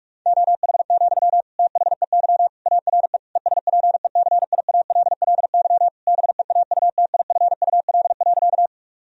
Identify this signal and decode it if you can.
Morse code, heard as a keyed tone, about 46 wpm, 700 Hz. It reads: OH, THEY ARE ESPECIALLY BEAUTIFUL.
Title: Sentences from Top 1000 Words - Review Part 4 - 45wpm